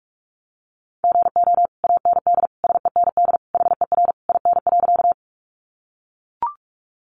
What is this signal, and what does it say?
Morse code, heard as a keyed tone, about 44 wpm, 700 Hz. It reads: GO AND SEND HER IN.
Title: Sentences from Top 1500 Words 45wpm